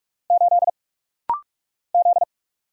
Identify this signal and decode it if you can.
Morse code, heard as a keyed tone, about 44 wpm, 700 Hz. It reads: 8 Z
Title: Single Letter-Number - Rapid-Fire 45wpm